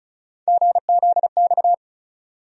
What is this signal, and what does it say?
Morse code, heard as a keyed tone, about 36 wpm, 700 Hz. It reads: GZX
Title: USA Call Sign Postfixes - 3 Letters 35wpm